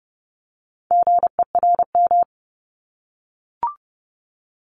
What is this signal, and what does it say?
Morse code, heard as a keyed tone, about 30 wpm, 690 Hz. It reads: GERM
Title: Multi-letter Phonemes 30wpm